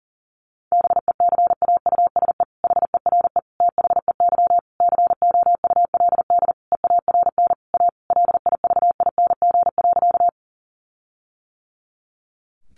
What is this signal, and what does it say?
Morse code, heard as a keyed tone, about 40 wpm, 700 Hz. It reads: BECAUSE HERE THEY COULD EARN A LIVING.
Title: Sentences from Top 1500 Words 40wpm